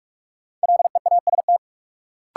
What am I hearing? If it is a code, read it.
Morse code, heard as a keyed tone, about 46 wpm, 700 Hz. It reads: LEAST
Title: Sets of 2 Words - New Word Drill - 45wpm